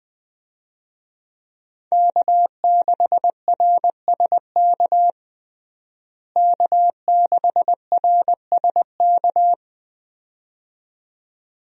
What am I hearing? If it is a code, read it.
Morse code, heard as a keyed tone, about 21 wpm, 700 Hz. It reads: K6RSK K6RSK
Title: USA Call Signs - 5 Characters - Repeated 2x - 20wpm